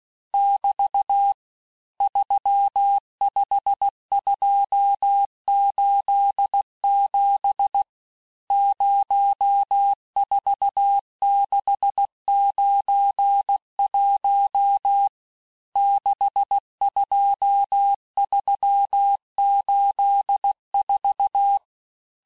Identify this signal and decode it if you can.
Morse code, heard as a keyed tone, about 16 wpm, 800 Hz. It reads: <BT> 35287 04691 62384
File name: cours-du-lundi files / Cours-1586-texte-chiffre-vitesse-15mpm